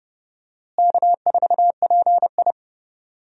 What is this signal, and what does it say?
Morse code, heard as a keyed tone, about 30 wpm, 700 Hz. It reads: K4PI
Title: USA Call Signs - 4 Characters - Repeated 2x - 30wpm